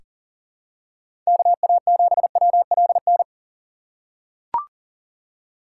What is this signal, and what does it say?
Morse code, heard as a keyed tone, about 40 wpm, 700 Hz. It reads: KA7WLN
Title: USA Call Signs - 6 Characters - 3x Character Spacing - 40wpm